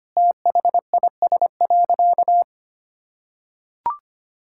Morse code, heard as a keyed tone, about 25 wpm, 700 Hz. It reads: THIS.
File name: practice / 25wpm / Sentences with 4-Word Phrases - 25wpm